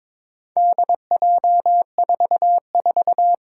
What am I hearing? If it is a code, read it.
Morse code, heard as a keyed tone, about 23 wpm, 700 Hz. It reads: DJ44
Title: Four Challenging Letter-Number Combinations 22wpm